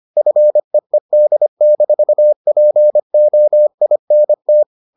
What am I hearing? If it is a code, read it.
Morse code, heard as a keyed tone, about 25 wpm, 600 Hz. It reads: FEED-POINT